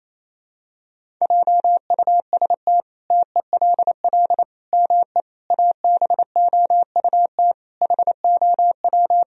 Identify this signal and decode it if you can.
Morse code, heard as a keyed tone, about 28 wpm, 700 Hz. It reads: JUST TELL ME ABOUT HOW
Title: Sentences from Top 1000 Words - Review Part 1 - 28wpm